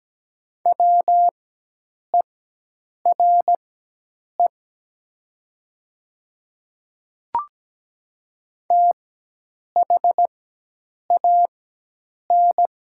Morse code, heard as a keyed tone, about 17 wpm, spaced as for 8 wpm, 700 Hz. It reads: WERE THAN
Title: 4x Character Spacing - Top 100 Words - 17wpm